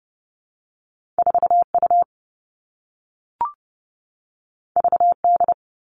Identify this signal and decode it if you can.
Morse code, heard as a keyed tone, about 30 wpm, 700 Hz. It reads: 4U VD
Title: All Cut Numbers - 2-Digit 30wpm